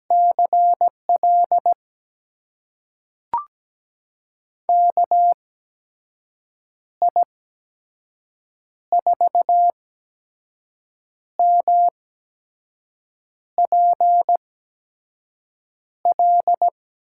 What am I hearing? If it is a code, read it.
Morse code, heard as a keyed tone, about 17 wpm, spaced as for 5 wpm, 700 Hz. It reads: CL KI4MPL
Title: USA Call Signs - 8x Character Spacing - 17wpm